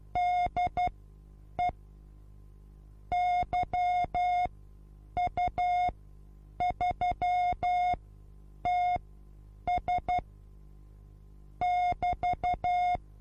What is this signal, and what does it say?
Morse code, heard as a keyed tone, about 12 wpm, spaced as for 8 wpm, 720 Hz. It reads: DE YU3TS <BT>
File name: repos / Morsekurs / DJ4UF Morsen23